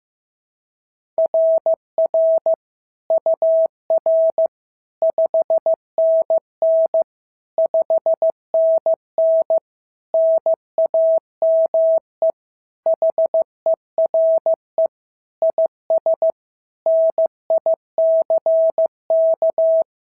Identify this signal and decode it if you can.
Morse code, heard as a keyed tone, about 14 wpm, 650 Hz. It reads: RR UR 5NN 5NN NAME HERE IS NICK
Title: SOTA QSO Practice 15wpm